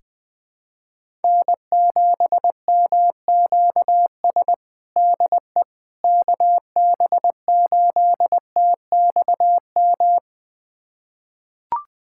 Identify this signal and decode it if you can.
Morse code, heard as a keyed tone, about 20 wpm, 700 Hz. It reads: N7MQS DE KB8TXM
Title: USA Callsign DE Callsign 20wpm